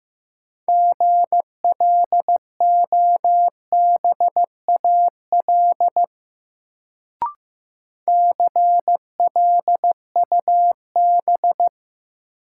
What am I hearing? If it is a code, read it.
Morse code, heard as a keyed tone, about 15 wpm, 700 Hz. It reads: GLOBAL CLUB
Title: Sentences from Top 1200 Words - New Word Drill Alt - 15wpm